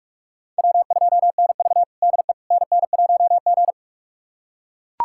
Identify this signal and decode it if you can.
Morse code, heard as a keyed tone, about 41 wpm, 700 Hz. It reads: W2NV DE NN1G
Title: USA Callsign DE Callsign - 4 Characters - 45wpm